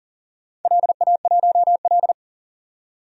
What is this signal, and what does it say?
Morse code, heard as a keyed tone, about 40 wpm, 700 Hz. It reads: LA1L
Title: DXCC-100 Call Signs - 4 Characters - 40wpm